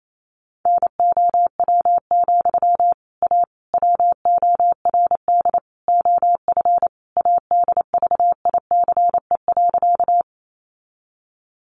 Morse code, heard as a keyed tone, about 28 wpm, 700 Hz. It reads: NOW, A WORD OF ADVICE.
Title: Sentences from Top 1100 Words 28wpm